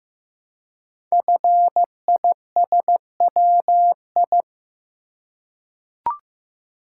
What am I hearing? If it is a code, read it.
Morse code, heard as a keyed tone, about 15 wpm, 700 Hz. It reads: FISWI
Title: Code Groups - 5 Letters - 15wpm